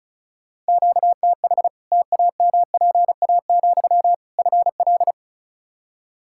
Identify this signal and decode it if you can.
Morse code, heard as a keyed tone, about 34 wpm, 700 Hz. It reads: QTH TAMPA, FL
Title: Practice QSO - Exchange 1 - Beginning - 35wpm